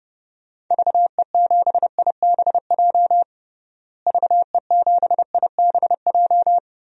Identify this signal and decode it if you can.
Morse code, heard as a keyed tone, about 32 wpm, 700 Hz. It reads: VE7IBJ VE7IBJ
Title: DXCC-100 Call Signs - 6 Characters - Repeated 3x - 30wpm